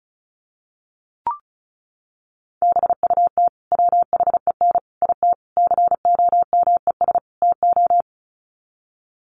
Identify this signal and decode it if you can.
Morse code, heard as a keyed tone, about 34 wpm, 700 Hz. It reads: BUT WHEN IT COMES TO
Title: Sets of 5 Words 35wpm